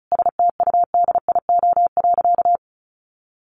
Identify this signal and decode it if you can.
Morse code, heard as a keyed tone, about 34 wpm, 700 Hz. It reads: STUDIO.
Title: Sentences from Top 1200 Words 35wpm